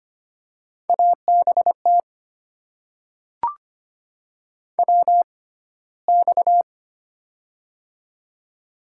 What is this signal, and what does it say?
Morse code, heard as a keyed tone, about 25 wpm, spaced as for 15 wpm, 700 Hz. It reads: ABT WX
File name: Top 90 QSO Elements - 6x Character Spacing - 25wpm